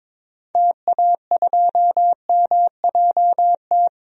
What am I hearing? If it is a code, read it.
Morse code, heard as a keyed tone, about 21 wpm, 700 Hz. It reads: TA2MJT
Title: DXCC-100 Call Signs - 6 Characters - 22wpm